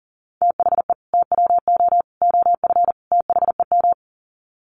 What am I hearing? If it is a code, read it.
Morse code, heard as a keyed tone, about 40 wpm, 700 Hz. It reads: THE TWO OF THEM
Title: Sets of 4 Words 40wpm